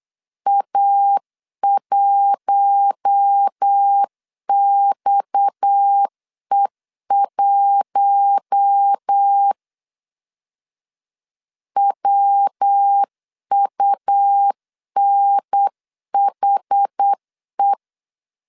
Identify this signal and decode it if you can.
Morse code, heard as a keyed tone, about 8 wpm, 800 Hz. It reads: A1XE1 WUNHE
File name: CW / DET / 1 6 1500C S C 8wpm 5219